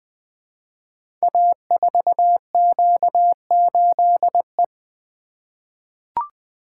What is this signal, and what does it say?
Morse code, heard as a keyed tone, about 20 wpm, 700 Hz. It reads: A4Q8E
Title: Code Groups - 5 Letter:Numbers - 20wpm